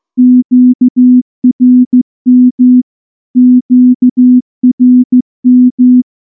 Morse code, heard as a keyed tone, about 14 wpm, 250 Hz. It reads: QRM QRM
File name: morse-training-15WPM